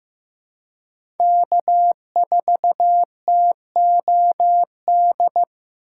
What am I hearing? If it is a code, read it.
Morse code, heard as a keyed tone, about 15 wpm, 700 Hz. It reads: K4TOD
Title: USA Call Signs - 5 Characters - 6x Character Spacing - 15wpm